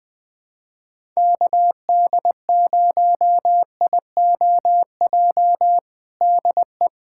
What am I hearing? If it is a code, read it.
Morse code, heard as a keyed tone, about 20 wpm, 700 Hz. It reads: KD0IOJ DE
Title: USA Callsign DE Callsign 20wpm